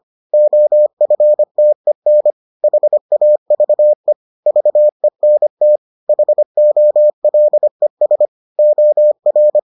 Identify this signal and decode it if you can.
Morse code, heard as a keyed tone, about 25 wpm, 600 Hz. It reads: OFTEN HAVE VENT HOLES OR